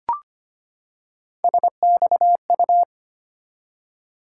Morse code, heard as a keyed tone, about 25 wpm, 700 Hz. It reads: SXU
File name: Three Letter-Number Combinations - 25wpm